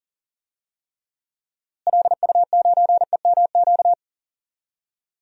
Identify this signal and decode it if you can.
Morse code, heard as a keyed tone, about 40 wpm, 700 Hz. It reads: LU9EMQ
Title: DXCC-100 Call Signs - 6 Characters - Repeated 2x - 40wpm